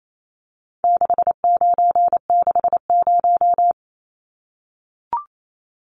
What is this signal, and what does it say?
Morse code, heard as a keyed tone, about 28 wpm, 700 Hz. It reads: 6960
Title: Numbers - 4-Digit 28wpm